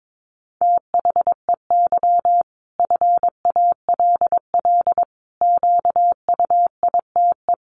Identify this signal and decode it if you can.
Morse code, heard as a keyed tone, about 22 wpm, 700 Hz. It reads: THEY FALL QUITE
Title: Sentences from Top 1000 Words 22wpm